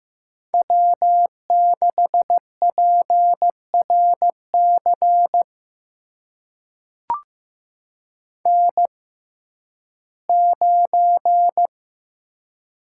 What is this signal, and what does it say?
Morse code, heard as a keyed tone, about 15 wpm, spaced as for 11 wpm, 700 Hz. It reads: W6PRC N9
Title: USA Call Signs - 5 Characters - 6x Character Spacing - 15wpm